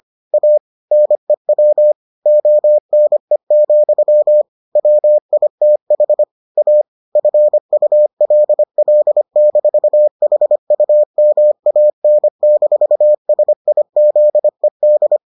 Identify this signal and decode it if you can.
Morse code, heard as a keyed tone, about 25 wpm, 600 Hz. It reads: A NEW ONE, WITH A FULL-HUMAN-SIZED